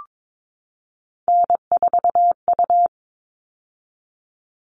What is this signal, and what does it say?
Morse code, heard as a keyed tone, about 22 wpm, 700 Hz. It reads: N4U
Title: All Cut Numbers - 3-Digit 22wpm